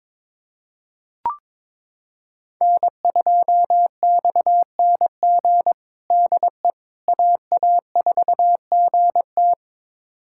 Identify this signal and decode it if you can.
Morse code, heard as a keyed tone, about 23 wpm, 700 Hz. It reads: N2XNG DE AA4GT
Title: USA Callsign DE Callsign - 5 Characters - 22wpm